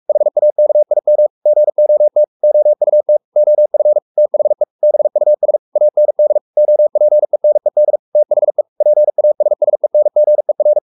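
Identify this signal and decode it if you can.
Morse code, heard as a keyed tone, about 44 wpm, 600 Hz. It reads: HAKIM GOT OUT OF THE BUS AND OPENED THE PASSENGER
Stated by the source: Morse code at 43 wpm sending Ken Follett's novel Never in English, from CW med Gnister.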